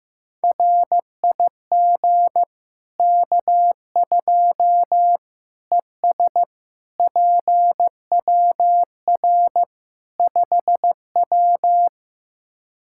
Morse code, heard as a keyed tone, about 15 wpm, 700 Hz. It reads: RIG K2 ES PWR 5W
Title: Practice QSO - Exchange 2 - Info - 15wpm